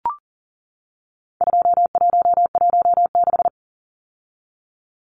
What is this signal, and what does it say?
Morse code, heard as a keyed tone, about 40 wpm, 700 Hz. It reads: 2116
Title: Numbers - 4-Digit 40wpm